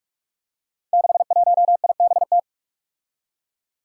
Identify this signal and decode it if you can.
Morse code, heard as a keyed tone, about 44 wpm, 700 Hz. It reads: 61IBT
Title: Five Challenging Letter-Number Combinations 45wpm